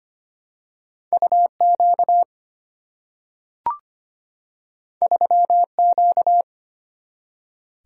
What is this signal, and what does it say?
Morse code, heard as a keyed tone, about 25 wpm, 700 Hz. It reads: UQ 3Q
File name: Two Letter-Number Combinations - 25wpm